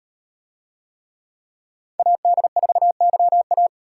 Morse code, heard as a keyed tone, about 38 wpm, 700 Hz. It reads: AD4YA